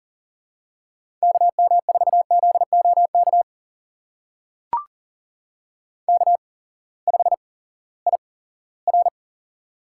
Morse code, heard as a keyed tone, about 40 wpm, spaced as for 20 wpm, 700 Hz. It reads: KM4ZOK K5IR
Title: USA Call Signs - 8x Character Spacing - 40wpm